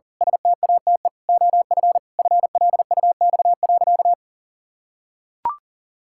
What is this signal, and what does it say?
Morse code, heard as a keyed tone, about 37 wpm, 700 Hz. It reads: STATE OF FLUX.
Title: Pangram Challenge v2 - 40wpm